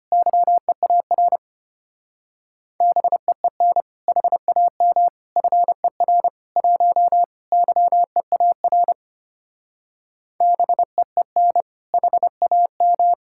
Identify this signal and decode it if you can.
Morse code, heard as a keyed tone, about 33 wpm, 700 Hz. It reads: YEAR BEEN HAM FER 1 YEAR BEEN HAM